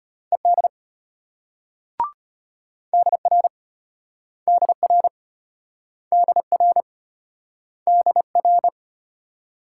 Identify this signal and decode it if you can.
Morse code, heard as a keyed tone, about 35 wpm, 700 Hz. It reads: ED DR DR DR DR